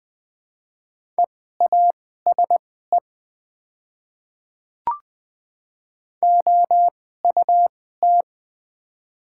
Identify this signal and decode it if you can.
Morse code, heard as a keyed tone, about 20 wpm, spaced as for 14 wpm, 700 Hz. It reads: EASE OUT
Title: 2x Character Spacing - Top 500 Words - 20wpm